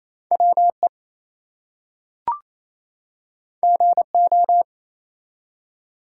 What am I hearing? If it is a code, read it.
Morse code, heard as a keyed tone, about 27 wpm, 700 Hz. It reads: WE GO
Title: Two Letter Words - 28wpm